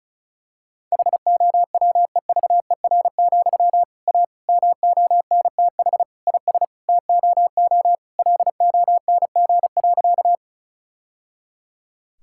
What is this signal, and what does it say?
Morse code, heard as a keyed tone, about 34 wpm, 700 Hz. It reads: HOWEVER, A MONTH IS TOO LONG.
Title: Sentences from Top 1000 Words - Review Part 5 - 35wpm